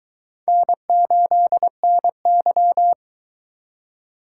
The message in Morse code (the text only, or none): N8NY